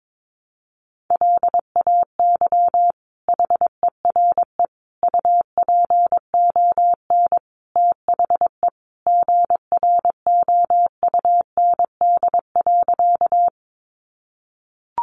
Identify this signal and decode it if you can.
Morse code, heard as a keyed tone, about 22 wpm, 700 Hz. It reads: LAY HERE UPON THE GROUND.
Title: Sentences from Top 1100 Words 22wpm